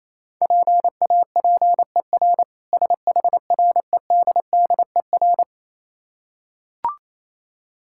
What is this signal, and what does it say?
Morse code, heard as a keyed tone, about 27 wpm, 700 Hz. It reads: PAPER SHREDDER
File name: Easy Words - Speed-Racing with Hints - 22wpm